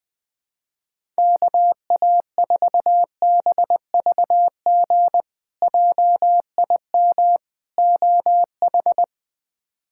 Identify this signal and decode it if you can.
Morse code, heard as a keyed tone, about 20 wpm, 700 Hz. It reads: KA4BVG JIM OH
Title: K1USN SST QSO Practice - 20wpm